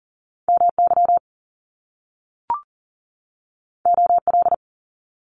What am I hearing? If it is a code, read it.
Morse code, heard as a keyed tone, about 38 wpm, 700 Hz. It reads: MY OL
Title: Top Two Letter Combinations in Words 40wpm